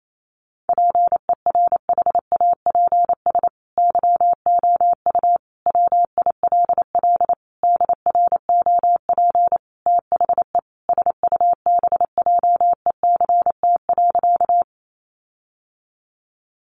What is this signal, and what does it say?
Morse code, heard as a keyed tone, about 28 wpm, 700 Hz. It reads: PERHAPS YOU WILL DROP THE SUBJECT.
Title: Sentences from Top 1400 Words 28wpm